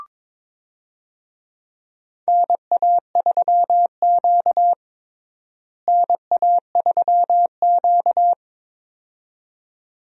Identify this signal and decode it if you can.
Morse code, heard as a keyed tone, about 22 wpm, 700 Hz. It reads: NA3Q NA3Q
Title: USA Call Signs - 4 Characters - Repeated 2x - 22wpm